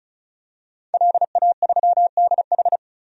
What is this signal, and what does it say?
Morse code, heard as a keyed tone, about 34 wpm, 700 Hz. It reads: LA3DH
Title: DXCC-100 Call Signs - 5 Characters - Repeated 2x - 35wpm